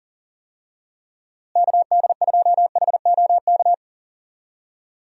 Morse code, heard as a keyed tone, about 40 wpm, 700 Hz. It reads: KD2HOK
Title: USA Call Signs - 6 Characters - Repeated 2x - 40wpm